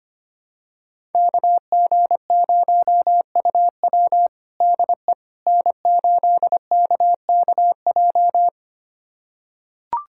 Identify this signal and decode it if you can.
Morse code, heard as a keyed tone, about 25 wpm, 700 Hz. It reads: KG0UW DE N8KKJ
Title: USA Callsign DE Callsign - 5 Characters - 25wpm